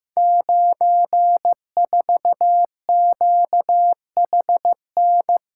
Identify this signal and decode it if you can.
Morse code, heard as a keyed tone, about 15 wpm, 700 Hz. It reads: 94QHN
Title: Code Groups - 5 Letter:Numbers - 15wpm